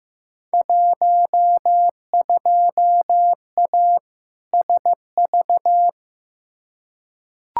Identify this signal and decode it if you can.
Morse code, heard as a keyed tone, about 15 wpm, 700 Hz. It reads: 12A SV
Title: Field Day Exchanges Encore - 15wpm